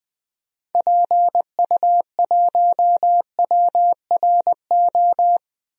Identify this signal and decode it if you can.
Morse code, heard as a keyed tone, about 20 wpm, 700 Hz. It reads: PU1WRO
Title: DXCC-100 Call Signs - 6 Characters - 20wpm